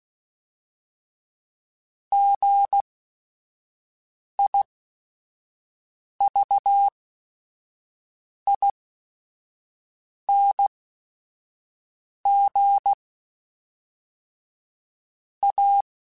Morse code, heard as a keyed tone, about 16 wpm, spaced as for 5 wpm, 800 Hz. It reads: GIVING A
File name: Technical Subjects / Technical Subjects-mp3s / XC @5